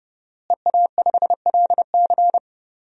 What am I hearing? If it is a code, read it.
Morse code, heard as a keyed tone, about 30 wpm, 700 Hz. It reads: EA5LC
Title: DXCC-100 Call Signs - 5 Characters - Repeated 2x - 30wpm